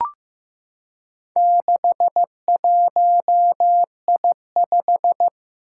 Morse code, heard as a keyed tone, about 15 wpm, 700 Hz. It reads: 61I5
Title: Four Challenging Letter-Number Combinations 15wpm